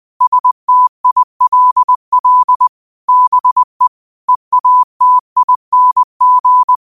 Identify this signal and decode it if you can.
Morse code, heard as a keyed tone, about 20 wpm, 1000 Hz. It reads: STILL BE EATING
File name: 20wpm